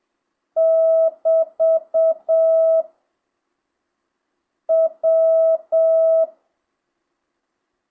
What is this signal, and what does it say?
Morse code, heard as a keyed tone, about 7 wpm, 1300 Hz, with noise. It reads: <BT> W
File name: jh8cbh / 2025cw